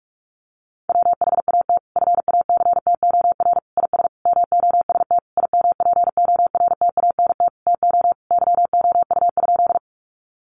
Morse code, heard as a keyed tone, about 46 wpm, 700 Hz. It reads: WHAT FACTOR IS MOST IMPORTANT TO YOU?
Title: Sentences from Top 1300 Words 45wpm